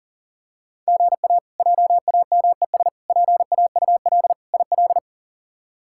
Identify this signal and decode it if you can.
Morse code, heard as a keyed tone, about 40 wpm, 700 Hz. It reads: GA JAMES PAUL IL
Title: K1USN SST QSO Practice - 40wpm